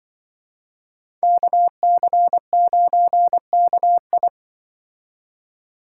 Morse code, heard as a keyed tone, about 25 wpm, 700 Hz. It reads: KC9KI
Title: Speed Racing - USA 2x2 Call Signs 20wpm